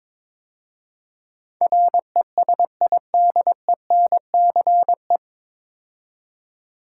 Morse code, heard as a keyed tone, about 22 wpm, 700 Hz. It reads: RESIDENCE
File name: 9 Letter Words - First 3 Letters Given - 22wpm